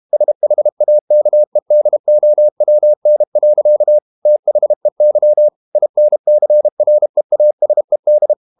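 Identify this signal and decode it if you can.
Morse code, heard as a keyed tone, about 32 wpm, 600 Hz. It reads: SHAKEDOWN. THEY INCREASED
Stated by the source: Morse code practice at 32 wpm from CW med Gnister, reading Ken Follett's novel Never in English.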